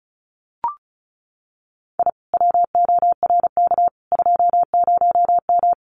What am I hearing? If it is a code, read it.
Morse code, heard as a keyed tone, about 34 wpm, 700 Hz. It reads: I WORK 20M
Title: Common QSO Phrases Encore 35wpm